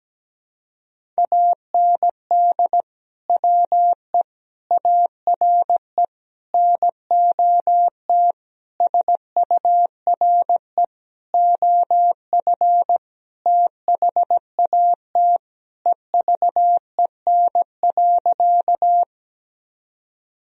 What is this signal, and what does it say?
Morse code, heard as a keyed tone, about 17 wpm, 700 Hz. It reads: AND WE ARE NOT SURE OF THAT EVEN.
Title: Sentences from Top 1000 Words - Review Part 1 - 17wpm